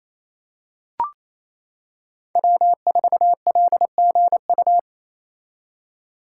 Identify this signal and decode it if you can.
Morse code, heard as a keyed tone, about 27 wpm, 700 Hz. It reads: W4LGU
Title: Code Groups - 5 Letter:Numbers - 28wpm